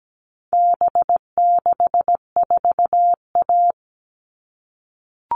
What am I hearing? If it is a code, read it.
Morse code, heard as a keyed tone, about 17 wpm, 700 Hz. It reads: B64A
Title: All Cut Numbers - 4-Digit 17wpm